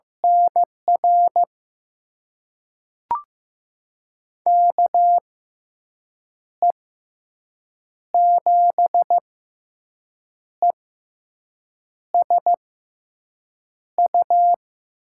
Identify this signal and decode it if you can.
Morse code, heard as a keyed tone, about 15 wpm, spaced as for 6 wpm, 700 Hz. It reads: NR KE7ESU
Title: USA Call Signs - 6 Characters - 6x Character Spacing - 15wpm